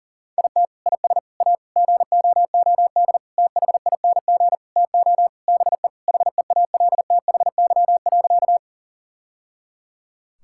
Morse code, heard as a keyed tone, about 40 wpm, 700 Hz. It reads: IT IS A GOOD THING TO BE HEALTHY.
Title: Sentences from Top 1000 Words - Review Part 5 - 40wpm